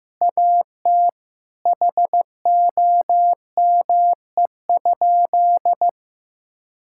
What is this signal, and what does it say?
Morse code, heard as a keyed tone, about 15 wpm, 700 Hz. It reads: AT HOME?
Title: Sentences from Top 1000 Words - Review Part 5 - 15wpm